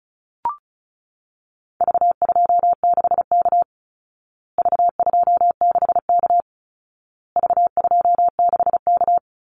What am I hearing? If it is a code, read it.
Morse code, heard as a keyed tone, about 34 wpm, 700 Hz. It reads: V26K V26K V26K
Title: Call Signs Repeated 2x - 35wpm